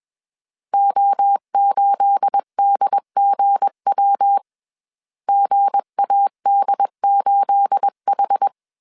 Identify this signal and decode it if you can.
Morse code, heard as a keyed tone, about 21 wpm, 800 Hz. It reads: O8DGW GAD8H
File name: CW / DET / 1 9 1500C S C 20wpm 2310